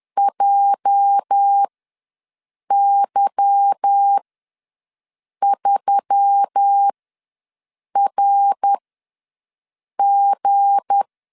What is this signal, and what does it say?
Morse code, heard as a keyed tone, about 10 wpm, spaced as for 6 wpm, 800 Hz. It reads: JY3RG
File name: LEZIONI / 8 YJ45 / 8 YJ ws 10wpm 4043